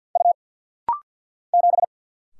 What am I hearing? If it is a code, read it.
Morse code, heard as a keyed tone, about 49 wpm, 700 Hz. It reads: U 7
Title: Single Letter-Number - Rapid-Fire 50wpm